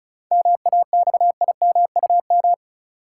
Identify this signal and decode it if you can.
Morse code, heard as a keyed tone, about 34 wpm, 700 Hz. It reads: MAXIMUM
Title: Sentences from Top 1300 Words - New Word Drill - 35wpm